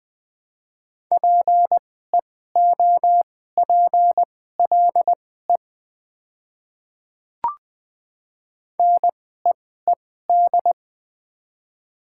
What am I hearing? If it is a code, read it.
Morse code, heard as a keyed tone, about 21 wpm, spaced as for 15 wpm, 700 Hz. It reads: PEOPLE NEED
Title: 2x Character Spacing - Top 200 Words - 20wpm